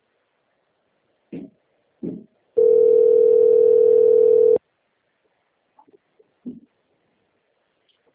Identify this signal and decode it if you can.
Ringback tone (North America), 440 + 480 Hz.